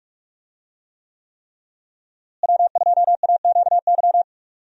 Morse code, heard as a keyed tone, about 46 wpm, 700 Hz. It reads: W2AQY
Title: USA Call Signs - 5 Characters - Repeated 3x - 45wpm